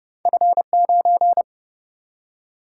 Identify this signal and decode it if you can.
Morse code, heard as a keyed tone, about 30 wpm, 700 Hz. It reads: F 9
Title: Single-Letter Single-Number 30wpm